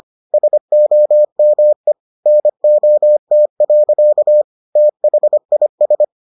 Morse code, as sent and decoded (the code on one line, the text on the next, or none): ... --- -- . / -. --- - .-.-.- / - .... .. ...
SOME NOT. THIS